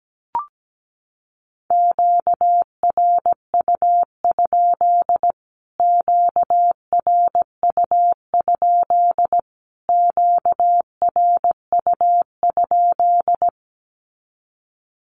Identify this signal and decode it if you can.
Morse code, heard as a keyed tone, about 17 wpm, 700 Hz. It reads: QRU? QRU? QRU?
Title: Q-Codes Repeated 3x 17wpm